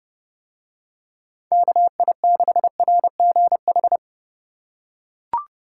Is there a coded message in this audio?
Yes: Morse code from 1.5 to 4.0 s, heard as a keyed tone, about 29 wpm, 700 Hz. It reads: KI6RGH